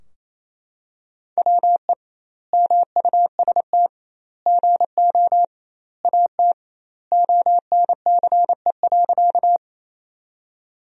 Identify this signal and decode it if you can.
Morse code, heard as a keyed tone, about 28 wpm, 700 Hz. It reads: WE MUST GO AT ONCE.
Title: 2x Word-Spacing - Sentences from Top 300 Words - 28wpm